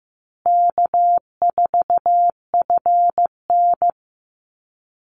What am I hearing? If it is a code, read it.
Morse code, heard as a keyed tone, about 15 wpm, 700 Hz. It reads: K4FN
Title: Easy Call Signs Repeated 3x - 15wpm